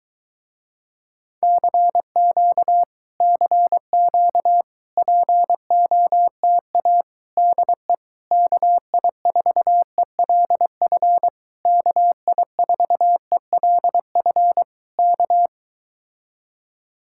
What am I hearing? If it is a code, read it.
Morse code, heard as a keyed tone, about 23 wpm, 700 Hz. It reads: CQ CQ POTA DE KI4ELF KI4ELF K